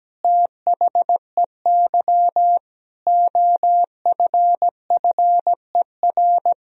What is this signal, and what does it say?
Morse code, heard as a keyed tone, about 17 wpm, 700 Hz. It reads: THEY OFFER
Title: Sentences from Top 1000 Words - Review Part 4 - 17wpm